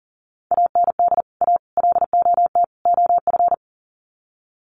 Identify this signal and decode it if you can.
Morse code, heard as a keyed tone, about 40 wpm, 700 Hz. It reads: AND A LOT OF
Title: Sets of 4 Words 40wpm